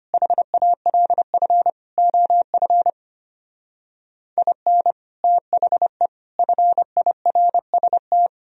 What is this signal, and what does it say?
Morse code, heard as a keyed tone, about 33 wpm, 700 Hz. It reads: HALF OF IN THE FIRST